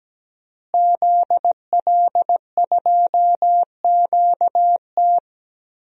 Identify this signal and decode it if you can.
Morse code, heard as a keyed tone, about 17 wpm, 700 Hz. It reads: ZL2QT
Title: DXCC-100 Call Signs - 5 Characters - Repeated 3x - 17wpm